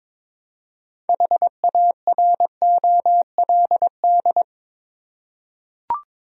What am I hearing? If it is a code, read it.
Morse code, heard as a keyed tone, about 22 wpm, 700 Hz. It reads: HAROLD
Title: USA Common Names - 22wpm